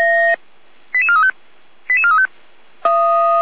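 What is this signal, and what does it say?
Five-tone selective call (ZVEI), heard as quick runs of five tones, tones of about 80 ms. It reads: 88435 88435
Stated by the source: ZVEI five-tone selective call received by SDR, from the feuerwehr-orlamuende.org audio archive.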